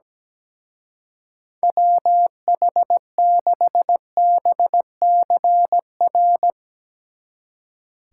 Morse code, heard as a keyed tone, about 17 wpm, 700 Hz. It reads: WH6BCR
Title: Speed Racing - USA 2x3 Call Signs 17wpm